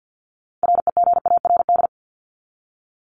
Morse code, heard as a keyed tone, about 50 wpm, 700 Hz. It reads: REGARD
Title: ED Suffix - Top 100 Words with Base Words 50wpm